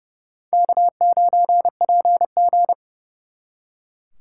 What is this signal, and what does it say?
Morse code, heard as a keyed tone, about 37 wpm, 700 Hz. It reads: K9PG
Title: Speed Racing - USA 1x2 Call Signs 30wpm